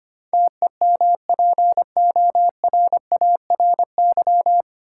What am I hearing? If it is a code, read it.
Morse code, heard as a keyed tone, about 25 wpm, 700 Hz. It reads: TEMPORARY
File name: Nine Letter Words - 25wpm